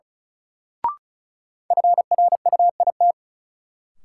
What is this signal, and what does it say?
Morse code, heard as a keyed tone, about 34 wpm, 700 Hz. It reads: FRUIT
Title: Sentences from Top 1300 Words - New Word Drill Alt - 35wpm